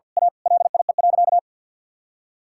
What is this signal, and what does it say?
Morse code, heard as a keyed tone, about 50 wpm, 700 Hz. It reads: A LIE.
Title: Sentences from Top 1600 Words 50wpm